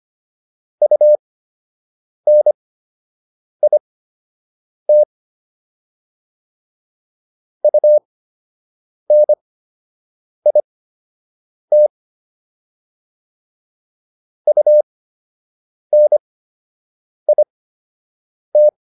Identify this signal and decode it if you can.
Morse code, heard as a keyed tone, about 25 wpm, spaced as for 7 wpm, 600 Hz. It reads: UNIT UNIT UNIT